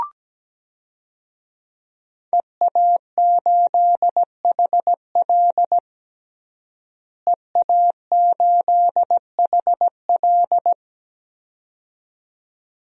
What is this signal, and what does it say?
Morse code, heard as a keyed tone, about 17 wpm, 700 Hz. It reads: EA8HL EA8HL
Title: DXCC-100 Call Signs - 5 Characters - Repeated 2x - 17wpm